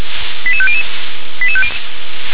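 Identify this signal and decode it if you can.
Five-tone selective call (ZVEI), heard as quick runs of five tones, tones of about 70 ms. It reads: 88500
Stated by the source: ZVEI five-tone call received by SDR.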